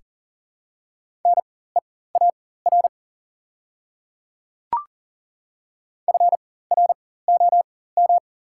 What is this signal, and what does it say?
Morse code, heard as a keyed tone, about 40 wpm, spaced as for 19 wpm, 690 Hz. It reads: NEAR FROM
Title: 4x Character Spacing - Top 200 Words - 40wpm